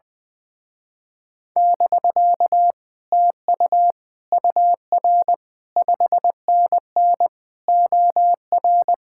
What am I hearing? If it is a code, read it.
Morse code, heard as a keyed tone, about 20 wpm, 700 Hz. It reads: <BK> TU UR 5NN OR
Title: Speed-Racing Common QSO Phrases - 20wpm